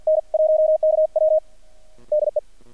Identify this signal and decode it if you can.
Morse code, heard as a keyed tone, about 48 wpm, 620 Hz. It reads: N1KW DE
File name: archived / CRSnet